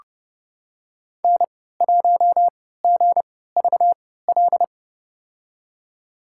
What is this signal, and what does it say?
Morse code, heard as a keyed tone, about 32 wpm, spaced as for 17 wpm, 690 Hz. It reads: N1GVL
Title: USA Call Signs - 3x Character Spacing - 30wpm